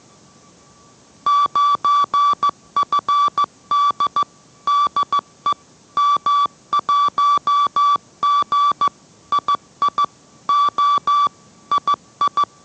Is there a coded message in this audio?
Yes: Morse code from 1.3 to 12.4 s, heard as a keyed tone, about 15 wpm, 1150 Hz. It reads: 9FD DE M1G II O II